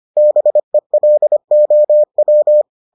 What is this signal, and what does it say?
Morse code, heard as a keyed tone, about 25 wpm, 600 Hz. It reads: BELOW